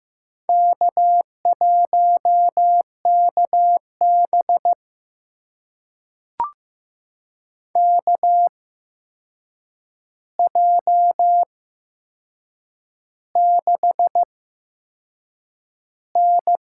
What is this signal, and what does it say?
Morse code, heard as a keyed tone, about 16 wpm, spaced as for 6 wpm, 700 Hz. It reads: K1KB KJ6N
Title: USA Call Signs - 8x Character Spacing - 15wpm